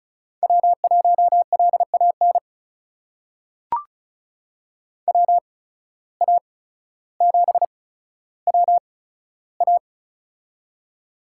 Morse code, heard as a keyed tone, about 36 wpm, spaced as for 15 wpm, 700 Hz. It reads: W1LAN WA7WA
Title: USA Call Signs - 5 Characters - 8x Character Spacing - 35wpm